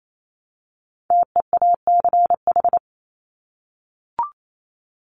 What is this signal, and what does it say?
Morse code, heard as a keyed tone, about 28 wpm, 700 Hz. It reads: TEACH
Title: Top 500 Words Repeated 28wpm